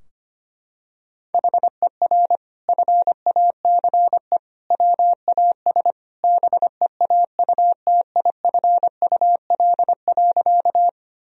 Morse code, heard as a keyed tone, about 25 wpm, 700 Hz. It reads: HER FACE WAS BEAUTIFUL.
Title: Sentences from Top 1000 Words - Review Part 2 - 25wpm